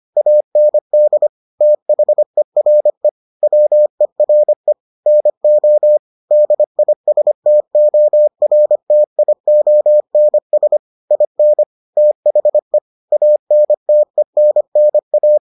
Morse code, heard as a keyed tone, about 25 wpm, 600 Hz. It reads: AND THERE WERE NO DISTORTIONS IN THE ANTENNA